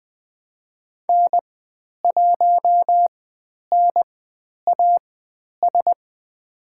Morse code, heard as a keyed tone, about 20 wpm, spaced as for 10 wpm, 700 Hz. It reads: N1NAS
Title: 095 Lesson 16g - Letter c - Call Signs 20wpm